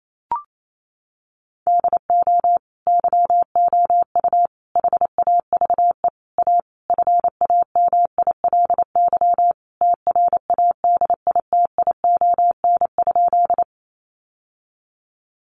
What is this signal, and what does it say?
Morse code, heard as a keyed tone, about 28 wpm, 700 Hz. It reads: DO YOU HAVE A FAMILY TRADITION?
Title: Sentences from Top 1500 Words 28wpm